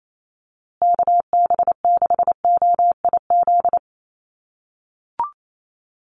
Morse code, heard as a keyed tone, about 28 wpm, 700 Hz. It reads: KB6OIZ
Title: USA Call Signs Repeated 2x 28wpm